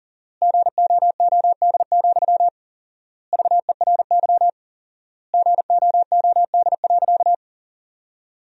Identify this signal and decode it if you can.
Morse code, heard as a keyed tone, about 40 wpm, 700 Hz. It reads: GOOD, VERY GOOD.
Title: 4x Word-Spacing - Sentences from Top 200 Words - 40wpm